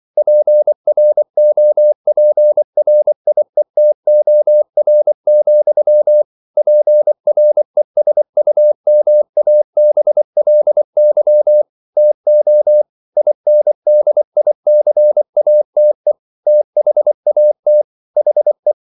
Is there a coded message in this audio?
Yes: Morse code from 0.2 to 18.7 s, heard as a keyed tone, about 24 wpm, 600 Hz. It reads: PROPRIETOR, PRESUMABLY TO INDICATE THAT HE